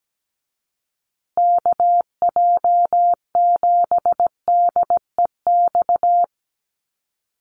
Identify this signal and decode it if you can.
Morse code, heard as a keyed tone, about 17 wpm, 700 Hz. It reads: KJ7DEX
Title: USA Call Signs Repeated 3x 17wpm